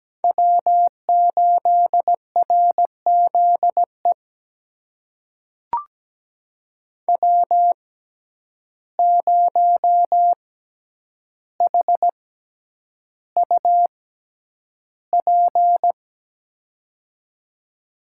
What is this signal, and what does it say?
Morse code, heard as a keyed tone, about 17 wpm, spaced as for 9 wpm, 700 Hz. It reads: W8RZE W0HUP
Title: USA Call Signs - 5 Characters - 6x Character Spacing - 17wpm